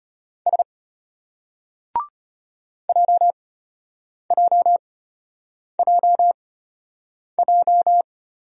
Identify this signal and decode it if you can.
Morse code, heard as a keyed tone, about 31 wpm, spaced as for 8 wpm, 700 Hz. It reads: S JJJJ
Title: Speed-Racing Single Letters 25wpm